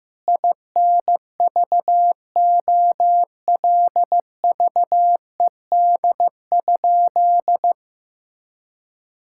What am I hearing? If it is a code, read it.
Morse code, heard as a keyed tone, about 15 wpm, 700 Hz. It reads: INVOLVED?
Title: Sentences from Top 1000 Words - Review Part 5 - 15wpm